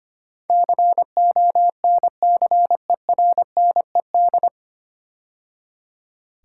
Morse code, heard as a keyed tone, about 25 wpm, 700 Hz. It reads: CONCERNED
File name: practice / 25wpm / 9 Letter Words - First 3 Letters Given - 25wpm